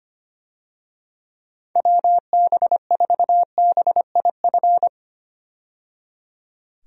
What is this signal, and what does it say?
Morse code, heard as a keyed tone, about 25 wpm, 700 Hz. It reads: WB4BIF
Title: USA Call Signs - 6 Characters - 25wpm